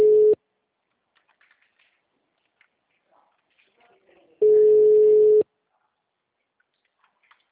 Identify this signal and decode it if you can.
Ringback tone (Europe, 425 Hz).